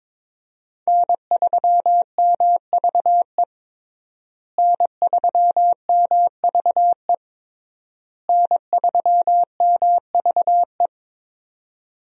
Morse code, heard as a keyed tone, about 22 wpm, 700 Hz. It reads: N3MVE N3MVE N3MVE
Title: USA Call Signs - 5 Characters - Repeated 3x - 22wpm